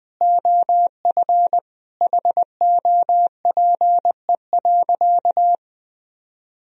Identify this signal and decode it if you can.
Morse code, heard as a keyed tone, about 20 wpm, 700 Hz. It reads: OF HOPE.
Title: Sentences from Top 1000 Words - Review Part 3 - 20wpm